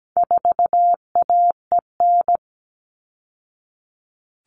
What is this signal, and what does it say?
Morse code, heard as a keyed tone, about 17 wpm, 700 Hz. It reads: 4AEN
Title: All Cut Numbers - 4-Digit 17wpm